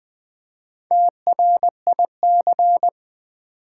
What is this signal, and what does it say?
Morse code, heard as a keyed tone, about 21 wpm, 700 Hz. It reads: TRIC
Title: Build A Word - 1 Letter At A Time - 3 to 5-Letter Words - 20wpm